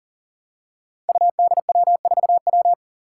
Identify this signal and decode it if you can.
Morse code, heard as a keyed tone, about 43 wpm, 700 Hz. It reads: UDW4W
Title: Five Challenging Letter-Number Combinations 40wpm